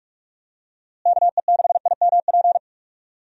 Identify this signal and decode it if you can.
Morse code, heard as a keyed tone, about 41 wpm, 700 Hz. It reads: KE6IMP
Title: USA Call Signs - 6 Characters - Repeated 3x - 45wpm